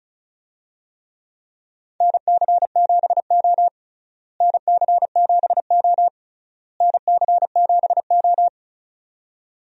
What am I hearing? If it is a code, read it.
Morse code, heard as a keyed tone, about 34 wpm, 700 Hz. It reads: NC7O NC7O NC7O
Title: USA Call Signs - 4 Characters - Repeated 3x - 35wpm